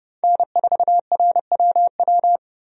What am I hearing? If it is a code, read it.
Morse code, heard as a keyed tone, about 30 wpm, 700 Hz. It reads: N4RWW
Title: USA Call Signs - 5 Characters - Repeated 2x - 30wpm